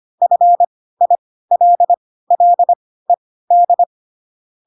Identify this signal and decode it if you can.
Morse code, heard as a keyed tone, about 25 wpm, spaced as for 16 wpm, 700 Hz. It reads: FILLED
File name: Pinocchio / Pinocchio ch1 10 25 ik1hge